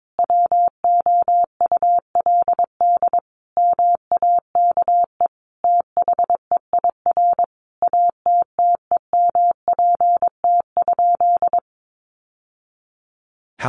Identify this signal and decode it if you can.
Morse code, heard as a keyed tone, about 22 wpm, 700 Hz. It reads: WOULD MAKE THEIR ATTEMPT?
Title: Sentences from Top 1000 Words 22wpm